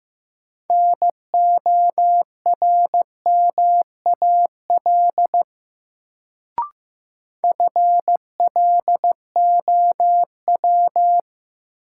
Morse code, heard as a keyed tone, about 15 wpm, 700 Hz. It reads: NORMAL FLOW
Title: Sentences from Top 1200 Words - New Word Drill - 15wpm